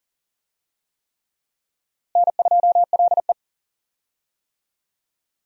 Morse code, heard as a keyed tone, about 38 wpm, 700 Hz. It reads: N2LE
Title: USA Call Signs - 4 Characters - Repeated 3x - 40wpm